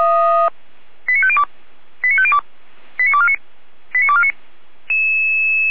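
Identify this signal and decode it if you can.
Five-tone selective call (ZVEI), heard as quick runs of five tones, tones of about 70 ms. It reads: 89502 89502 89249 89249
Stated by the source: ZVEI five-tone selective call received by SDR.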